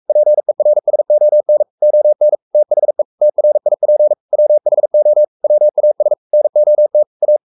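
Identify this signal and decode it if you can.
Morse code, heard as a keyed tone, about 44 wpm, 600 Hz. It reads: PERSON ON THE TRIP WHO WAS NOT A
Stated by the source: Morse code at 43 wpm sending Ken Follett's novel Never in English, from CW med Gnister.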